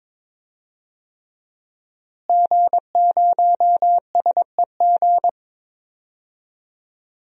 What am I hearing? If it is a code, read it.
Morse code, heard as a keyed tone, about 22 wpm, 700 Hz. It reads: G0SEG
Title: DXCC-100 Call Signs - 5 Characters - Repeated 3x - 22wpm